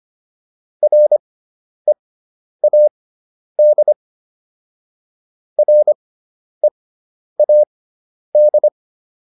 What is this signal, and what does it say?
Morse code, heard as a keyed tone, about 25 wpm, spaced as for 10 wpm, 600 Hz. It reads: READ READ